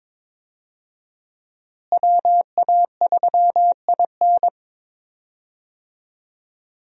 Morse code, heard as a keyed tone, about 22 wpm, 700 Hz. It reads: WA3IN
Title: USA Call Signs - 5 Characters - Repeated 2x - 22wpm